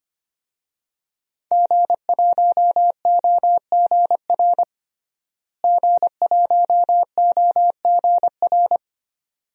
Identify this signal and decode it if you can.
Morse code, heard as a keyed tone, about 25 wpm, 700 Hz. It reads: G1OGR G1OGR
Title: DXCC-100 Call Signs - 5 Characters - Repeated 2x - 25wpm